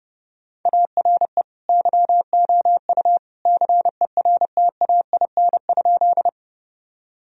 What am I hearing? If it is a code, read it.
Morse code, heard as a keyed tone, about 30 wpm, 700 Hz. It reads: ARE YOU CERTAIN?
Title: Sentences from Top 1000 Words - Review Part 2 - 30wpm